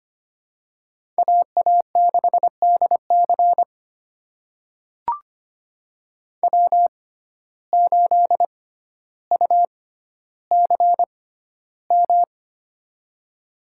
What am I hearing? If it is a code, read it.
Morse code, heard as a keyed tone, about 25 wpm, spaced as for 13 wpm, 700 Hz. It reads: AA6DC W8UCM
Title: USA Call Signs - 5 Characters - 6x Character Spacing - 25wpm